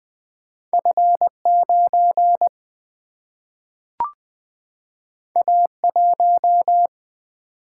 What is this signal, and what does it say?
Morse code, heard as a keyed tone, about 20 wpm, 700 Hz. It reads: F9 A1
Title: Single-Letter Single-Number 20wpm